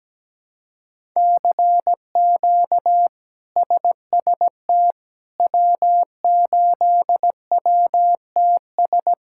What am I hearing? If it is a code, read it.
Morse code, heard as a keyed tone, about 17 wpm, 700 Hz. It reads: CQ SST W8WTS
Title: K1USN SST QSO Practice - 17wpm